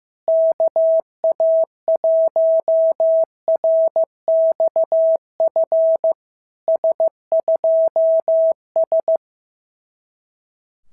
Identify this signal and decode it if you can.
Morse code, heard as a keyed tone, about 14 wpm, 650 Hz. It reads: KA1RXF S2S
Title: SOTA QSO Practice 15wpm